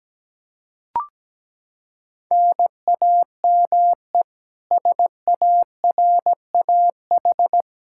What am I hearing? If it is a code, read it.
Morse code, heard as a keyed tone, about 17 wpm, 700 Hz. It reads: NAME SARAH
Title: K1USN SST QSO Practice 17wpm